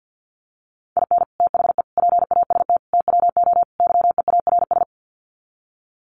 Morse code, heard as a keyed tone, about 50 wpm, 700 Hz. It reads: IN THE PAST TWO YEARS
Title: Sets of 5 Words 50wpm